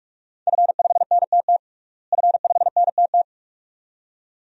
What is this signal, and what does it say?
Morse code, heard as a keyed tone, about 46 wpm, 700 Hz. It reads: F5NTT F5NTT
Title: DXCC-100 Call Signs - 5 Characters - Repeated 2x - 45wpm